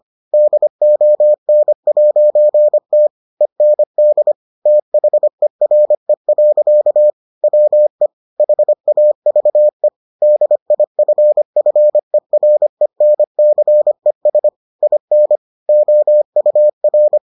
Morse code, heard as a keyed tone, about 25 wpm, 600 Hz. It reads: DON'T END THERE. WE HAVE DIFFERENCES IN OUR